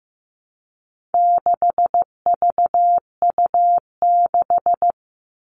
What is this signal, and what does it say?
Morse code, heard as a keyed tone, about 15 wpm, 700 Hz. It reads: 6VU6
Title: All Cut Numbers - 4-Digit 15wpm